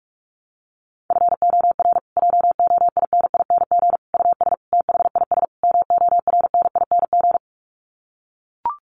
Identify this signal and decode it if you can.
Morse code, heard as a keyed tone, about 44 wpm, 700 Hz. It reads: FOR JOINING US THIS MORNING
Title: Sets of 5 Words Encore 45wpm